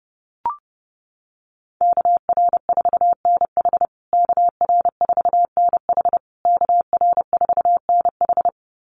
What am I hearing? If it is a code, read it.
Morse code, heard as a keyed tone, about 30 wpm, 700 Hz. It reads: KR4NH KR4NH KR4NH
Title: USA Call Signs Repeated 3x 30wpm